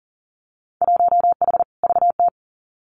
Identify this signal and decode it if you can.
Morse code, heard as a keyed tone, about 42 wpm, 700 Hz. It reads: 1H VT
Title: Winter Field Day Exchanges Encore 40wpm